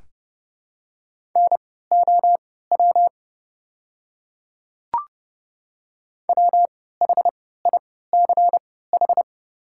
Morse code, heard as a keyed tone, about 30 wpm, spaced as for 17 wpm, 700 Hz. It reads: NOW WHICH
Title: 3x Character Spacing - Top 100 Words - 30wpm